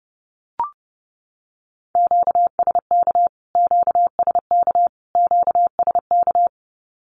Morse code, heard as a keyed tone, about 30 wpm, 700 Hz. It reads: QSK QSK QSK
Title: Q-Codes Repeated 3x 30wpm